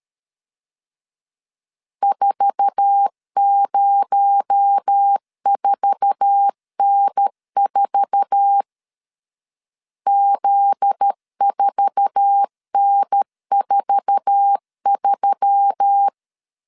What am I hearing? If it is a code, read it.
Morse code, heard as a keyed tone, about 13 wpm, 800 Hz. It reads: 404N4 Z4N43
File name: LEZIONI / 8 YJ45 / 8 YJ4 12wpm 2151